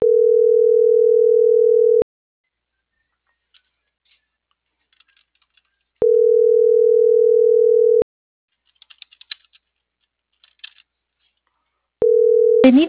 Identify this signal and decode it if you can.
Ringback tone (North America), 440 + 480 Hz.